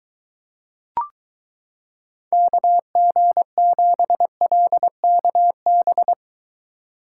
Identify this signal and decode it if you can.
Morse code, heard as a keyed tone, about 22 wpm, 700 Hz. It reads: KG7LKB